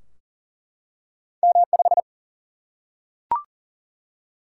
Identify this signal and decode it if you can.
Morse code, heard as a keyed tone, about 40 wpm, 700 Hz. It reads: M5
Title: Single-Letter Single-Number 40wpm